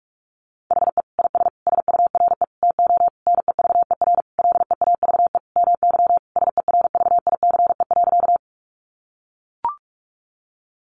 Morse code, heard as a keyed tone, about 44 wpm, 700 Hz. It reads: HE IS SURE TO NEVER LEAVE MY SERVICE.
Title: Sentences from Top 700 Words 45wpm